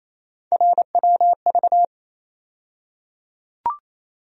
Morse code, heard as a keyed tone, about 27 wpm, 700 Hz. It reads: RWV
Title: USA Call Sign Postfixes - 3 Letters 28wpm